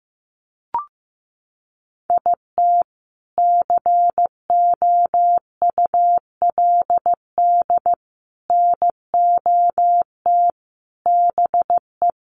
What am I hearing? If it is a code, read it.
Morse code, heard as a keyed tone, about 15 wpm, 700 Hz. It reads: IT COULD NOT BE
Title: Sentences from Top 1500 Words 15wpm